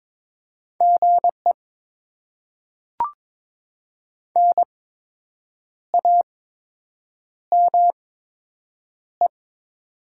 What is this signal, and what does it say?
Morse code, heard as a keyed tone, about 22 wpm, spaced as for 7 wpm, 700 Hz. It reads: GE NAME
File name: Top 20 QSO Elements - 8x Character Spacing - 22wpm